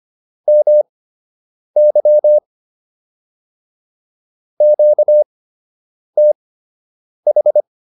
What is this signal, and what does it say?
Morse code, heard as a keyed tone, about 25 wpm, spaced as for 8 wpm, 600 Hz. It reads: MY QTH